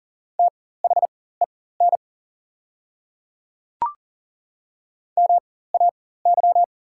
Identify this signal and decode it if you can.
Morse code, heard as a keyed tone, about 40 wpm, spaced as for 19 wpm, 700 Hz. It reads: THEN MAY
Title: 4x Character Spacing - Top 300 Words - 40wpm